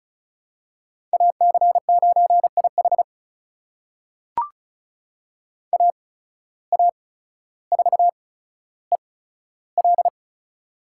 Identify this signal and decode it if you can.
Morse code, heard as a keyed tone, about 36 wpm, spaced as for 15 wpm, 700 Hz. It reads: AC9IH AA4EL
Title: USA Call Signs - 5 Characters - 8x Character Spacing - 35wpm